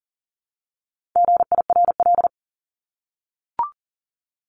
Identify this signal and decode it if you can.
Morse code, heard as a keyed tone, about 40 wpm, 700 Hz. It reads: GIRL
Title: Top 500 Words Repeated 40wpm